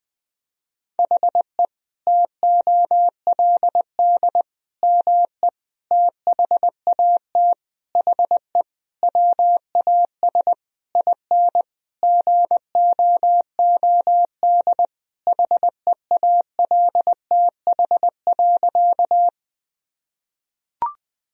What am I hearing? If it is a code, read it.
Morse code, heard as a keyed tone, about 20 wpm, 700 Hz. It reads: HE TOLD ME THAT HE WAS IN GOOD HEALTH.
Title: Sentences from Top 1000 Words - Review Part 3 - 20wpm